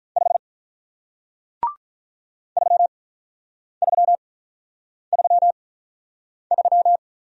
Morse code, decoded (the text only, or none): 5 3333